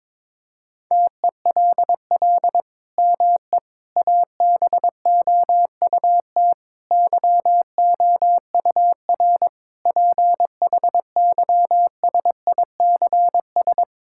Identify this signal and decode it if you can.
Morse code, heard as a keyed tone, about 21 wpm, 700 Hz. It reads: TELL ME ABOUT YOUR PHYSICS
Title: Sentences from Top 1000 Words - Review Part 5 - 22wpm